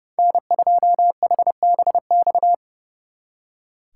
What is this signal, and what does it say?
Morse code, heard as a keyed tone, about 35 wpm, 700 Hz. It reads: N2HBX
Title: Speed Racing - USA 1x3 Call Signs 30wpm